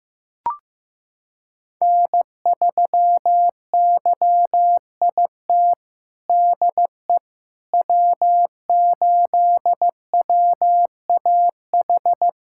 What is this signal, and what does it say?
Morse code, heard as a keyed tone, about 15 wpm, 700 Hz. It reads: N3YIT DE W8WAH
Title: USA Callsign DE Callsign - 5 Characters - 15wpm